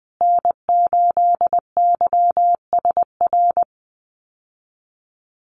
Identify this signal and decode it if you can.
Morse code, heard as a keyed tone, about 20 wpm, 700 Hz. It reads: N8YSR
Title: USA Call Signs Repeated 3x 20wpm